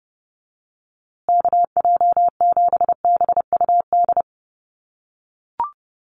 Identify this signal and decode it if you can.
Morse code, heard as a keyed tone, about 30 wpm, 700 Hz. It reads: KJ7BUD
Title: USA Call Signs Repeated 2x 30wpm